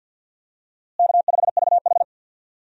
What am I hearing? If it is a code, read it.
Morse code, heard as a keyed tone, about 45 wpm, 700 Hz. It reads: K5VH
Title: USA Call Signs - 4 Characters - Repeated 3x - 50wpm